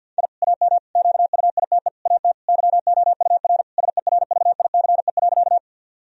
Morse code, heard as a keyed tone, about 50 wpm, 700 Hz. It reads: I AM QUITE AT YOUR SERVICE.
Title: Sentences from Top 1000 Words - Review Part 4 - 50wpm